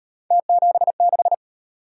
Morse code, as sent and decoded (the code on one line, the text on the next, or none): - --... -....
T76